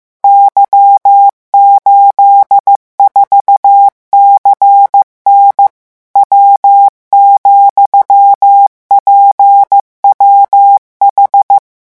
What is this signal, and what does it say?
Morse code, heard as a keyed tone, about 15 wpm, 800 Hz. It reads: Y84CN W,PWH
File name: morse mp3 / revision13-vitesse 14 mots